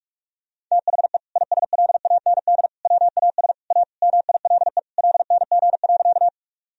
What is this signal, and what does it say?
Morse code, heard as a keyed tone, about 46 wpm, 700 Hz. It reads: THE ISLAND WAS A MILE LNG.
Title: Sentences from Top 500 Words and Abbreviations 45wpm